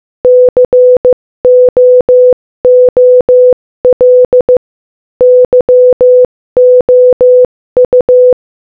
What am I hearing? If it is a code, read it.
Morse code, heard as a keyed tone, about 15 wpm, 500 Hz. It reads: COOL YOU
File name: Morse-Code-Audio-Speed-15-6